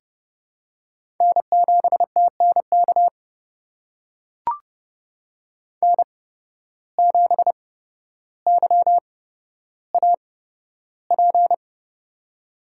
Morse code, heard as a keyed tone, about 30 wpm, spaced as for 13 wpm, 700 Hz. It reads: N7TNK N7YAP
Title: USA Call Signs - 5 Characters - 8x Character Spacing - 30wpm